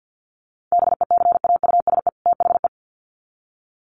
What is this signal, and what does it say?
Morse code, heard as a keyed tone, about 50 wpm, 700 Hz. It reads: BECAUSE THE
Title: Sets of 2 Words 50wpm